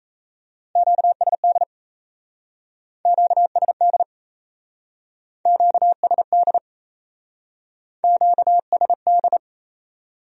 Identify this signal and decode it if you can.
Morse code, heard as a keyed tone, about 35 wpm, 700 Hz. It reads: QSD QSD QSD QSD
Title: Speed-Racing Q-Codes - 28wpm